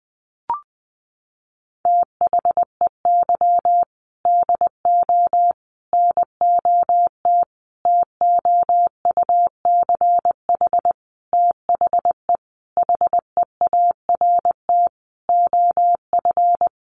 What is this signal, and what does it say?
Morse code, heard as a keyed tone, about 20 wpm, 700 Hz. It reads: THEY DO NOT TOUCH THE HEART OF
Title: Sentences from Top 1200 Words 20wpm